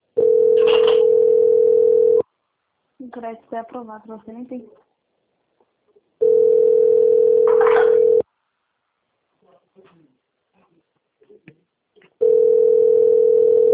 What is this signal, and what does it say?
Ringback tone (North America), 440 + 480 Hz, with speech.
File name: out- - - - - .21614